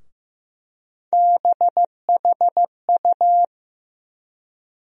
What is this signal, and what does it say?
Morse code, heard as a keyed tone, about 15 wpm, 700 Hz. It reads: BHU
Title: USA Call Sign Postfixes - 3 Letters 15wpm